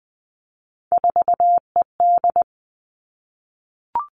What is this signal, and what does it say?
Morse code, heard as a keyed tone, about 20 wpm, 700 Hz. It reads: 4ED
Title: Three Challenging Letter-Number Combinations 20wpm